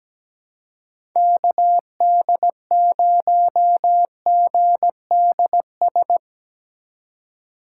Morse code, heard as a keyed tone, about 17 wpm, 700 Hz. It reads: KD0GDS
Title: USA Call Signs - 6 Characters - 4x Character Spacing - 17wpm